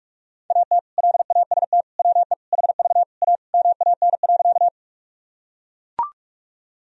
Morse code, heard as a keyed tone, about 43 wpm, 700 Hz. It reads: AT LAST WE HV A MAN.
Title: Sentences from Top 300 Words and Abbreviations 45wpm